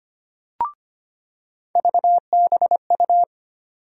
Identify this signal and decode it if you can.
Morse code, heard as a keyed tone, about 25 wpm, 700 Hz. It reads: VBU
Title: USA Call Sign Postfixes - 3 Letters 25wpm